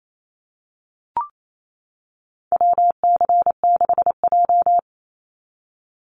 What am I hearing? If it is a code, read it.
Morse code, heard as a keyed tone, about 28 wpm, 700 Hz. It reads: WC6J
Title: NA SOTA Activator Callsigns 28wpm